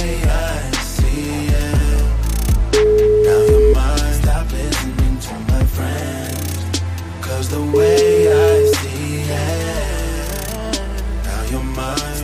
Ringback tone (Europe, 425 Hz), with noise.